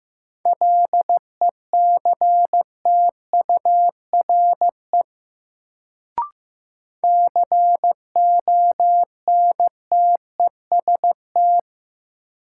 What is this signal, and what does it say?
Morse code, heard as a keyed tone, about 15 wpm, 700 Hz. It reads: LECTURE CONTEST
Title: Sentences from Top 1600 Words - New Word Drill - 15wpm